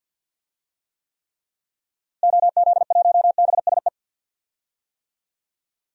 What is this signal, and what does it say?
Morse code, heard as a keyed tone, about 46 wpm, 700 Hz. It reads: OZ1BSE
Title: DXCC-100 Call Signs - 6 Characters - Repeated 3x - 50wpm